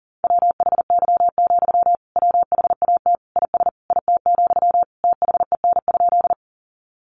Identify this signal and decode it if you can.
Morse code, heard as a keyed tone, about 40 wpm, 700 Hz. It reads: WHY, WHAT IS IT, THEN?
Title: Sentences from Top 200 Words 40wpm